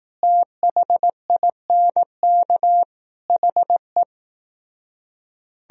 Morse code, heard as a keyed tone, about 18 wpm, 700 Hz. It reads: THINK HE
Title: Speed-Racing - Sets of 5 Words 15wpm